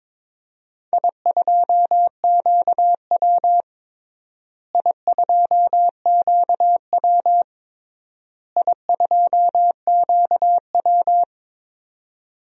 Morse code, heard as a keyed tone, about 22 wpm, 700 Hz. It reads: I2QW I2QW I2QW
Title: DXCC-100 Call Signs - 4 Characters - Repeated 3x - 22wpm